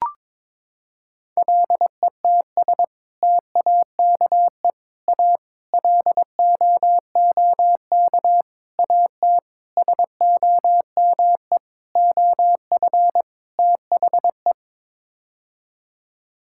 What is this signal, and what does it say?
Morse code, heard as a keyed tone, about 22 wpm, 700 Hz. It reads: LETS TAKE A LOOK AT SOME OF THE
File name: practice / 22wpm / Sets of 8 Words - 22wpm